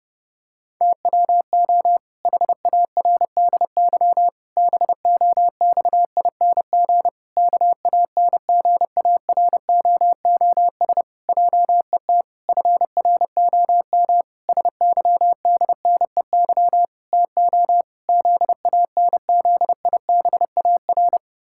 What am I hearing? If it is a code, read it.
Morse code, heard as a keyed tone, about 30 wpm, 700 Hz. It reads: TWO HARDY BOXING KANGAROOS JET FROM SYDNEY TO ZANZIBAR
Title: Pangram Challenge v2 - 30wpm